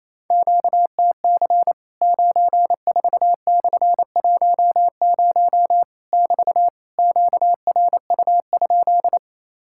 Morse code, heard as a keyed tone, about 28 wpm, 700 Hz. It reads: QTC 94/10 <BT> QRU?
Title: Worked All Europe DX Contest - QTC Exchange - 28wpm